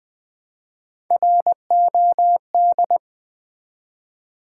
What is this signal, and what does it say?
Morse code, heard as a keyed tone, about 20 wpm, 700 Hz. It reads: ROD
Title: Build A Word - 1 Letter At A Time - 3 to 5-Letter Words - 20wpm